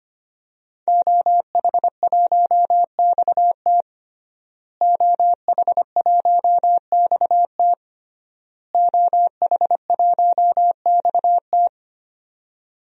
Morse code, heard as a keyed tone, about 25 wpm, 700 Hz. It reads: OH1XT OH1XT OH1XT
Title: DXCC-100 Call Signs - 5 Characters - Repeated 3x - 25wpm